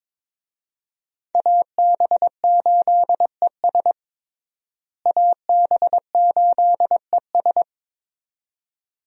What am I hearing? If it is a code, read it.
Morse code, heard as a keyed tone, about 22 wpm, 700 Hz. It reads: AB8ES AB8ES
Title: USA Call Signs - 5 Characters - Repeated 2x - 22wpm